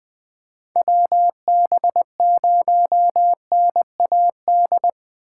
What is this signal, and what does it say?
Morse code, heard as a keyed tone, about 20 wpm, 700 Hz. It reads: WB0NAD
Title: USA Call Signs - 6 Characters - 4x Character Spacing - 20wpm